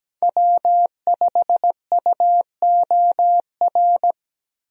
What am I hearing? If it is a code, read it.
Morse code, heard as a keyed tone, about 17 wpm, 700 Hz. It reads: W5UOR
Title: USA Call Signs - 5 Characters - Repeated 3x - 17wpm